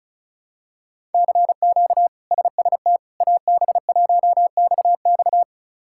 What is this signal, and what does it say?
Morse code, heard as a keyed tone, about 36 wpm, 700 Hz. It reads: CQ SST AB1XX
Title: K1USN SST QSO Practice - 35wpm